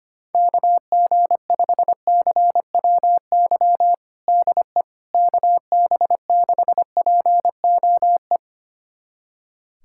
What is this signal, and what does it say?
Morse code, heard as a keyed tone, about 25 wpm, 700 Hz. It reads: KG5CWY DE KB6POE
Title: USA Callsign DE Callsign - 6 Characters - 25wpm